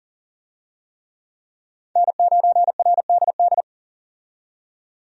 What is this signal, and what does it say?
Morse code, heard as a keyed tone, about 40 wpm, 700 Hz. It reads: N9RDD
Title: USA Call Signs - 5 Characters - 40wpm